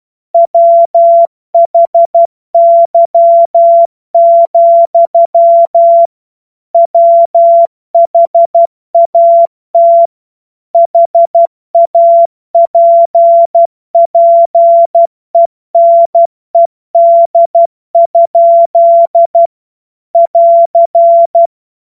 Morse code, heard as a keyed tone, about 12 wpm, 680 Hz. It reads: WHY, WHAT HAPPENED? <AR>
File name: Morsekurs / Witze